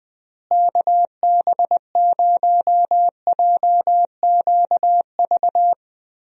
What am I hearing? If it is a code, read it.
Morse code, heard as a keyed tone, about 20 wpm, 700 Hz. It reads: KB0JQV
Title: USA Call Signs - 6 Characters - Repeated 3x - 20wpm